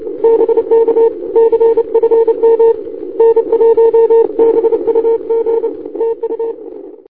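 Morse code, heard as a keyed tone, about 29 wpm, 460 Hz. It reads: BK CFM N1BUG TU
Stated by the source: Morse code heard on 160 m from 5A7A.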